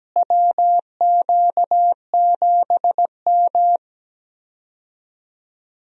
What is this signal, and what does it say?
Morse code, heard as a keyed tone, about 17 wpm, 700 Hz. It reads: WQ7M
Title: USA Call Signs - 4 Characters - Repeated 3x - 17wpm